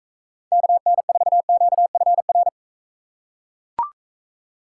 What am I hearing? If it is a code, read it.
Morse code, heard as a keyed tone, about 43 wpm, 700 Hz. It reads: KN4QFR